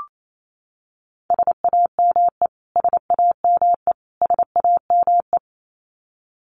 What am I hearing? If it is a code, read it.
Morse code, heard as a keyed tone, about 28 wpm, 700 Hz. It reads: SAME SAME SAME
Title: Top 500 Words Repeated 28wpm